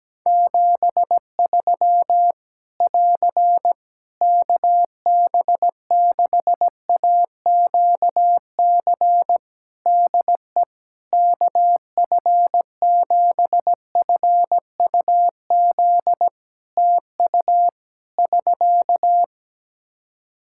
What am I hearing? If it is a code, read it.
Morse code, heard as a keyed tone, about 17 wpm, 700 Hz. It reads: 73 <AR> KB6AQC DE KF7FUZ TU <SK>
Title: Practice QSO - Exchange 4 - Ending - 17wpm